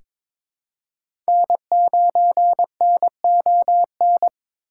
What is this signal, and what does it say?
Morse code, heard as a keyed tone, about 23 wpm, 700 Hz. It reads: N9NON
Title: USA Call Signs - 5 Characters - 3x Character Spacing - 22wpm